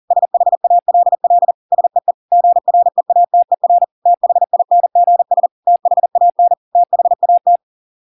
Morse code, heard as a keyed tone, about 40 wpm, 700 Hz. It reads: SHALL SEE GREATER THINGS THAN THAT